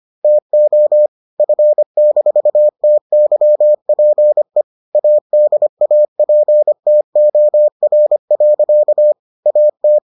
Morse code, heard as a keyed tone, about 25 wpm, 600 Hz. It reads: TO F-TYPE ADAPTOR. AT